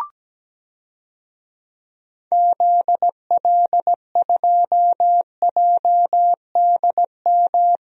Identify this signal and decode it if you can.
Morse code, heard as a keyed tone, about 17 wpm, 700 Hz. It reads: ZL2JDM
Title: DXCC-100 Call Signs - 6 Characters - 17wpm